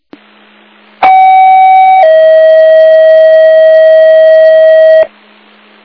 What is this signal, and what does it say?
Two-tone page, heard as a one-second tone, then a three-second tone, about 760 then 640 Hz. It is heard from 1.0 to 5.0 s.